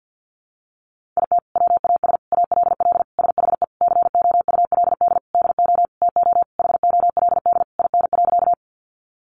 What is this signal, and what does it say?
Morse code, heard as a keyed tone, about 48 wpm, 700 Hz. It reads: IT WAS ALL SHE COULD DO TO HOLD IN.
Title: Sentences from Top 400 Words 50wpm